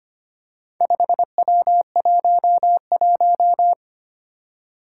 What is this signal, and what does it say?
Morse code, heard as a keyed tone, about 25 wpm, 700 Hz. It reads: 5W11
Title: Four Challenging Letter-Number Combinations 25wpm